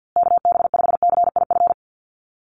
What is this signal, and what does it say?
Morse code, heard as a keyed tone, about 50 wpm, 700 Hz. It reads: KB5CIF
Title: USA Call Signs Repeated 3x 50wpm